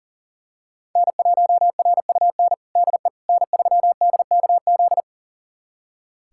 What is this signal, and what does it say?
Morse code, heard as a keyed tone, about 37 wpm, 700 Hz. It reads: N1RUN DE N3DKZ
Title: USA Callsign DE Callsign - 5 Characters - 40wpm